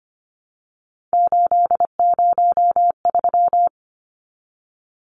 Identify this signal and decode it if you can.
Morse code, heard as a keyed tone, about 25 wpm, 700 Hz. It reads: 803
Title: Numbers - 3-Digit 25wpm